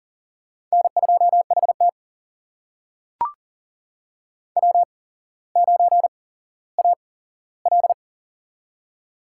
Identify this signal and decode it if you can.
Morse code, heard as a keyed tone, about 40 wpm, spaced as for 17 wpm, 700 Hz. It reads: N2HT W9AL
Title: USA Call Signs - 4 Characters - 8x Character Spacing - 40wpm